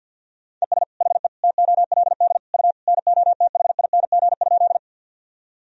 Morse code, heard as a keyed tone, about 47 wpm, 700 Hz. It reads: ES HE TOLD U NOTHING?
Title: Sentences from Top 500 Words and Abbreviations 50wpm